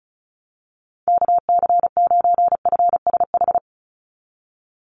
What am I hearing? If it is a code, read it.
Morse code, heard as a keyed tone, about 36 wpm, 700 Hz. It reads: KC9FSH
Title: USA Call Signs 35wpm